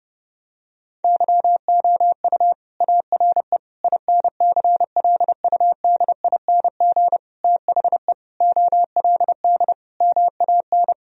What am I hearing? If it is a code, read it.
Morse code, heard as a keyed tone, about 32 wpm, 700 Hz. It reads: YOU ARE INCLUDING THE OLD MAN
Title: Sentences from Top 1000 Words - Review Part 4 - 30wpm